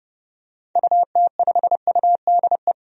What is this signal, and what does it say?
Morse code, heard as a keyed tone, about 31 wpm, 700 Hz. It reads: UT5UDE
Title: DXCC-100 Call Signs - 6 Characters - 30wpm